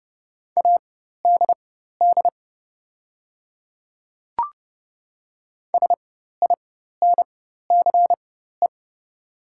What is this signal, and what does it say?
Morse code, heard as a keyed tone, about 31 wpm, spaced as for 14 wpm, 700 Hz. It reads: ADD SINCE
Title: 4x Character Spacing - Top 300 Words - 30wpm